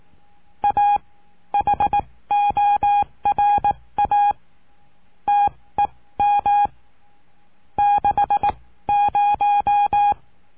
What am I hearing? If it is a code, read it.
Morse code, heard as a keyed tone, about 18 wpm, spaced as for 15 wpm, 820 Hz. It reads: A HORA TEM 60
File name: cursos / CURSO DE TELEGRAFIA / Telegrafia - Trilha 20